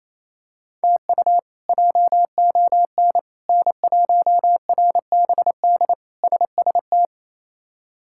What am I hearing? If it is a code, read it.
Morse code, heard as a keyed tone, about 28 wpm, 700 Hz. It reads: TU JON N1RBD SST
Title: K1USN SST QSO Practice - 28wpm